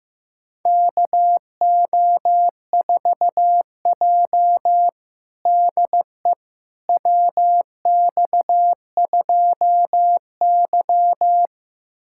Morse code, heard as a keyed tone, about 15 wpm, 700 Hz. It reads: KO4J DE WX2Y
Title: USA Callsign DE Callsign - 4 Characters - 15wpm